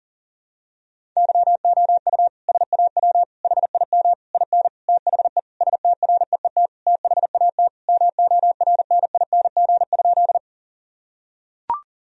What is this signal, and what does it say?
Morse code, heard as a keyed tone, about 37 wpm, 700 Hz. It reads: YOU SAW HIM IN THE STREET THAT MORNING?
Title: Sentences from Top 1000 Words - Review Part 3 - 40wpm